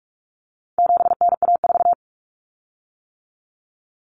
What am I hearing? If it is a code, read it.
Morse code, heard as a keyed tone, about 44 wpm, 700 Hz. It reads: 7NA4
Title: Common Cut Numbers - 4-Digit 45wpm